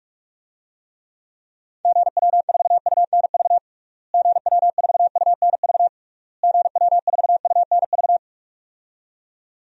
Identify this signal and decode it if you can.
Morse code, heard as a keyed tone, about 48 wpm, 700 Hz. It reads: GW4UNV GW4UNV GW4UNV
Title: DXCC-100 Call Signs - 6 Characters - Repeated 3x - 45wpm